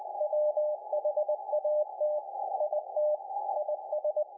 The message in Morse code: WHAT IT IS